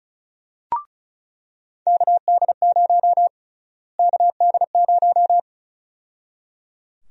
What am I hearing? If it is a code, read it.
Morse code, heard as a keyed tone, about 36 wpm, 700 Hz. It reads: KD0 KD0
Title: USA Call Sign Prefixes - Repeated 2x - 35wpm